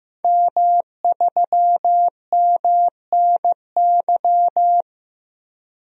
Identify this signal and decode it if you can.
Morse code, heard as a keyed tone, about 15 wpm, 700 Hz. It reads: M3MNY
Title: DXCC-100 Call Signs - 5 Characters - Repeated 3x - 15wpm